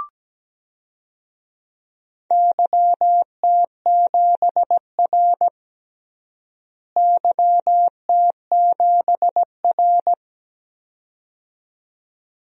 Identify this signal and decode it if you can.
Morse code, heard as a keyed tone, about 17 wpm, 700 Hz. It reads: YT7R YT7R
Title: DXCC-100 Call Signs - 4 Characters - Repeated 2x - 17wpm